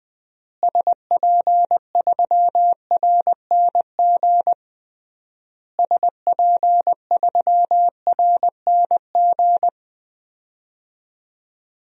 Morse code, heard as a keyed tone, about 20 wpm, 700 Hz. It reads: SP3RNG SP3RNG
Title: DXCC-100 Call Signs - 6 Characters - Repeated 3x - 20wpm